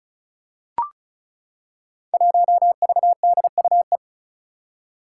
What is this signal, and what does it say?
Morse code, heard as a keyed tone, about 36 wpm, 700 Hz. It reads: 1VDUE
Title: Five Challenging Letter-Number Combinations 35wpm